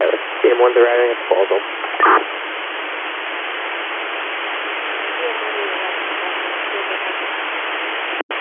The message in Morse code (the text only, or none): none